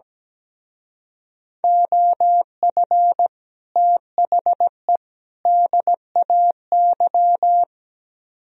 Morse code, heard as a keyed tone, about 18 wpm, 700 Hz. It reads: OF THE DAY
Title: Speed-Racing - Sets of 3 Words 17wpm